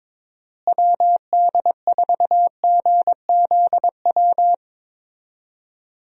Morse code, heard as a keyed tone, about 22 wpm, 700 Hz. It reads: WD4GZW
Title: USA Call Signs - 6 Characters - Repeated 3x - 22wpm